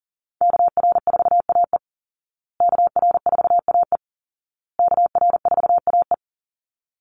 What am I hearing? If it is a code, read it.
Morse code, heard as a keyed tone, about 40 wpm, 700 Hz. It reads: KR4AE KR4AE KR4AE
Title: Call Signs Repeated 2x - 40wpm